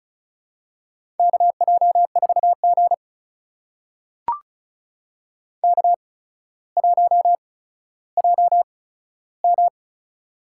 Morse code, heard as a keyed tone, about 36 wpm, spaced as for 15 wpm, 700 Hz. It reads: KJ4G K1JM
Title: USA Call Signs - 4 Characters - 8x Character Spacing - 35wpm